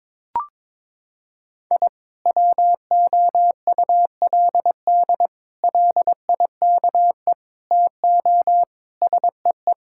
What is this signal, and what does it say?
Morse code, heard as a keyed tone, about 22 wpm, 700 Hz. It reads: I WOULD LIKE TO SEE
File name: practice / 22wpm / Sentences with 4-Word Phrases - 22wpm